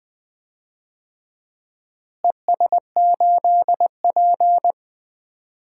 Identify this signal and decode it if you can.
Morse code, heard as a keyed tone, about 20 wpm, 700 Hz. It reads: ES8P
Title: DXCC-100 Call Signs - 4 Characters - Repeated 3x - 20wpm